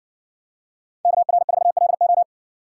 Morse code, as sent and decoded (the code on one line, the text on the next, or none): -.- .-. ....- .-.. ---
KR4LO